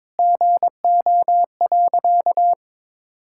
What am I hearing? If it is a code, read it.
Morse code, heard as a keyed tone, about 22 wpm, 700 Hz. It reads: GO.
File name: practice / 22wpm / Sentences with 2-Word Phrases - 22wpm